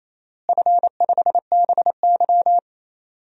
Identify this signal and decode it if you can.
Morse code, heard as a keyed tone, about 28 wpm, 700 Hz. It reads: F5BY
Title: DXCC-100 Call Signs - 4 Characters - Repeated 3x - 28wpm